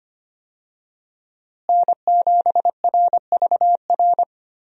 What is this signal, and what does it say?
Morse code, heard as a keyed tone, about 25 wpm, 700 Hz. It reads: N7RVR
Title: USA Call Signs - 5 Characters - Repeated 2x - 25wpm